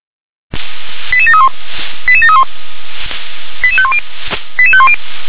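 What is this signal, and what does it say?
Five-tone selective call (ZVEI), heard as quick runs of five tones, tones of about 70 ms. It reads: 88731 88731 88510 88510
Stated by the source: ZVEI five-tone call received by SDR.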